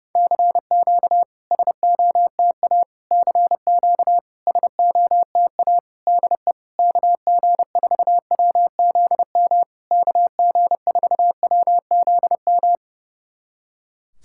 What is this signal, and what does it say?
Morse code, heard as a keyed tone, about 31 wpm, 700 Hz. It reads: CQ SOTA CQ SOTA DE KG4WZM KG4WZM
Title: SOTA QSO Practice 30wpm